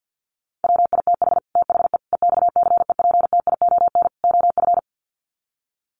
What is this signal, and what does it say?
Morse code, heard as a keyed tone, about 50 wpm, 700 Hz. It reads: WITH THE EXCEPTION OF
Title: Sets of 4 Words 50wpm